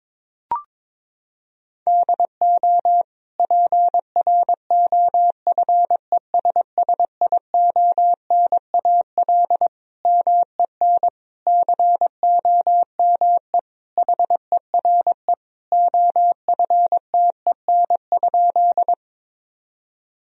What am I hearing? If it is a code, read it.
Morse code, heard as a keyed tone, about 22 wpm, 700 Hz. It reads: DO PROFESSIONAL MEN COME HERE OFTEN?
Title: Sentences from Top 1000 Words - Review Part 4 - 22wpm